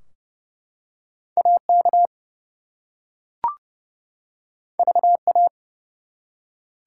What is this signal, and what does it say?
Morse code, heard as a keyed tone, about 30 wpm, 700 Hz. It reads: AK VA
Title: US States and Canadian Provinces 30wpm